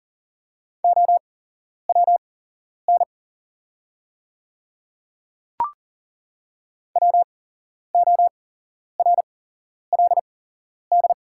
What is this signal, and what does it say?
Morse code, heard as a keyed tone, about 40 wpm, spaced as for 11 wpm, 700 Hz. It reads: OWN WORLD
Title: 8x Character Spacing - Top 300 Words - 40wpm